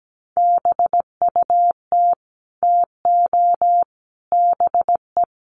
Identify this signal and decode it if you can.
Morse code, heard as a keyed tone, about 17 wpm, 700 Hz. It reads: BUT TO BE
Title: Sentences from Top 1200 Words 17wpm